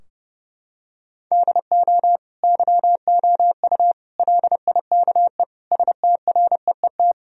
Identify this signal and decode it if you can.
Morse code, heard as a keyed tone, about 30 wpm, 700 Hz. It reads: DO YOU LIKE STREET
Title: Sentences from Top 1000 Words - Review Part 3 - 30wpm